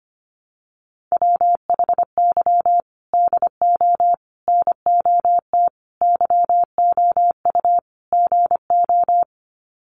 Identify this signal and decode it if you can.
Morse code, heard as a keyed tone, about 25 wpm, 700 Hz. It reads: WHY DO NOT YOU GO
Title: Sets of 5 Words Encore 25wpm